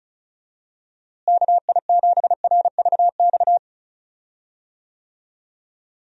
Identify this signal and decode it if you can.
Morse code, heard as a keyed tone, about 36 wpm, 700 Hz. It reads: KI7RVX
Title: USA Call Signs - 6 Characters - Repeated 2x - 35wpm